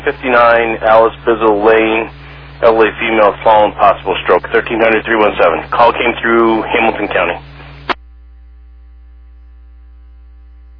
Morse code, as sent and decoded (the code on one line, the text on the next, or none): none